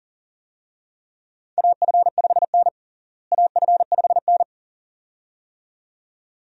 Morse code, heard as a keyed tone, about 40 wpm, 700 Hz. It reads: AF5N AF5N
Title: USA Call Signs - 4 Characters - Repeated 2x - 40wpm